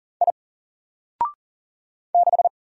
Morse code, heard as a keyed tone, about 40 wpm, 700 Hz. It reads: I 6
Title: Single Character (Letter:Number:Punctuation:Prosign) 40wpm